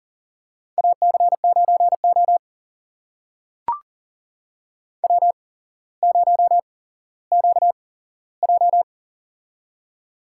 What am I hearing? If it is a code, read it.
Morse code, heard as a keyed tone, about 40 wpm, spaced as for 17 wpm, 700 Hz. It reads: AC9O W0QJ
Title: USA Call Signs - 4 Characters - 8x Character Spacing - 40wpm